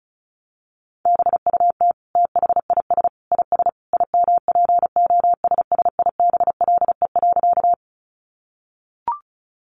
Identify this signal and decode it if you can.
Morse code, heard as a keyed tone, about 34 wpm, 700 Hz. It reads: BUT THIS IS IMPOSSIBLE.
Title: Sentences from Top 1100 Words 35wpm